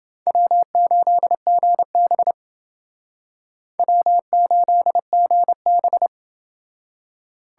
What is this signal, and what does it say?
Morse code, heard as a keyed tone, about 29 wpm, 700 Hz. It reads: W8GB W8GB